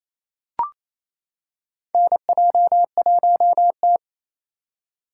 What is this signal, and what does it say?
Morse code, heard as a keyed tone, about 28 wpm, 700 Hz. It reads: NJ1T
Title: K1USN SST QSO Practice - 28wpm